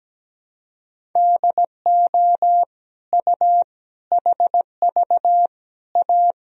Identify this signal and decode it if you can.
Morse code, heard as a keyed tone, about 17 wpm, 700 Hz. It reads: DO U HV A
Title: Sentences from Top 300 Words and Abbreviations 17wpm